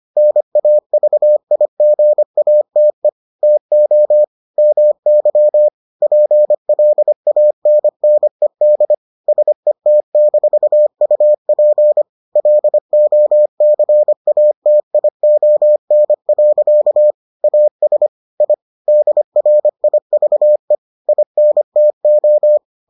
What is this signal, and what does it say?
Morse code, heard as a keyed tone, about 25 wpm, 600 Hz. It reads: NAVIGATE TO MY PLANNED SET-UP LOCATION. AS I DRIVE INTO